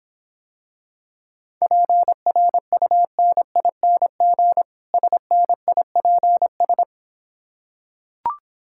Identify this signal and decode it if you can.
Morse code, heard as a keyed tone, about 26 wpm, 700 Hz. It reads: PRUNING SNIPS